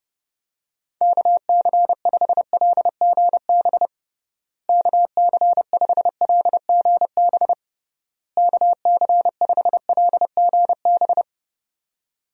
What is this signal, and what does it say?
Morse code, heard as a keyed tone, about 30 wpm, 700 Hz. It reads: KC5LGB KC5LGB KC5LGB
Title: USA Call Signs - 6 Characters - Repeated 3x - 30wpm